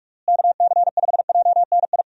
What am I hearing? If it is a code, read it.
Morse code, heard as a keyed tone, about 46 wpm, 700 Hz. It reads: KC5JNI
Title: USA Call Signs - 6 Characters - Repeated 3x - 45wpm